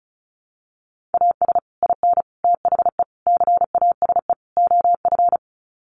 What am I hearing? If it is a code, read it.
Morse code, heard as a keyed tone, about 34 wpm, 700 Hz. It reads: AS IN THE CASE OF
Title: Sets of 5 Words 35wpm